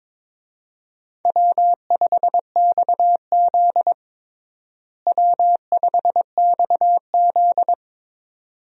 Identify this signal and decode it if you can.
Morse code, heard as a keyed tone, about 22 wpm, 700 Hz. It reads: W5XZ W5XZ
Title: USA Call Signs - 4 Characters - Repeated 2x - 22wpm